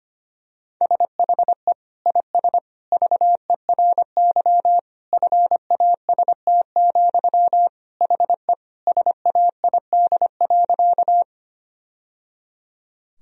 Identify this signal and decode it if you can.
Morse code, heard as a keyed tone, about 25 wpm, 700 Hz. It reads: SHE IS VERY FAST, HE SAID.
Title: Sentences from Top 1000 Words - Review Part 2 - 25wpm